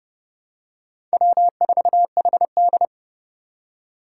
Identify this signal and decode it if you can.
Morse code, heard as a keyed tone, about 30 wpm, 700 Hz. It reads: W4HD
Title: Four Challenging Letter-Number Combinations 30wpm